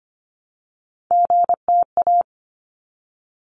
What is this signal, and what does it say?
Morse code, heard as a keyed tone, about 24 wpm, 700 Hz. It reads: GTA
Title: ARRL Sections 25wpm